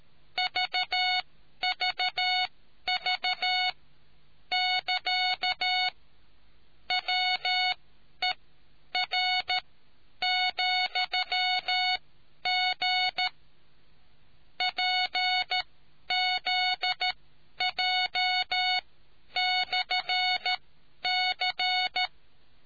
Morse code, heard as a keyed tone, about 13 wpm, spaced as for 10 wpm, 740 Hz. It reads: VVV <KA> WER,G PZJ/C